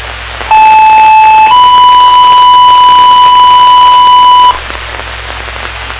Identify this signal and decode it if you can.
Two-tone page, heard as a one-second tone, then a three-second tone, about 870 then 990 Hz, with noise. It is heard from 0.5 to 4.5 s.